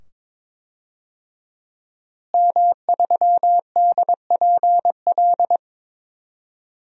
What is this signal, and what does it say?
Morse code, heard as a keyed tone, about 22 wpm, 700 Hz. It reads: M3DPL
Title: DXCC-100 Call Signs - 5 Characters - Repeated 2x - 22wpm